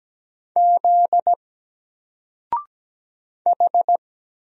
Morse code, heard as a keyed tone, about 17 wpm, 700 Hz. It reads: Z H
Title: Single Character (Letter:Number:Punctuation:Prosign) 17wpm